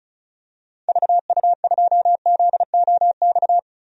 Morse code, heard as a keyed tone, about 36 wpm, 700 Hz. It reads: VU2ZOX
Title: DXCC-100 Call Signs - 6 Characters - Repeated 2x - 35wpm